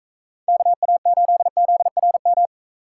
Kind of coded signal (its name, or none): Morse code